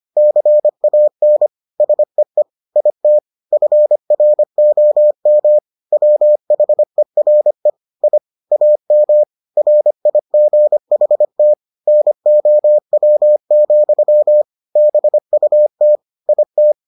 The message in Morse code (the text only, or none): CAN SEE IT FROM WHERE I AM RIGHT NOW, BUT IT